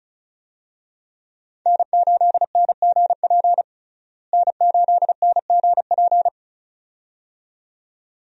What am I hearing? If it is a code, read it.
Morse code, heard as a keyed tone, about 34 wpm, 700 Hz. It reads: N8NGP N8NGP
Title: USA Call Signs - 5 Characters - Repeated 2x - 35wpm